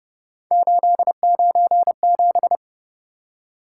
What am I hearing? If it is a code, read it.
Morse code, heard as a keyed tone, about 29 wpm, 700 Hz. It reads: 897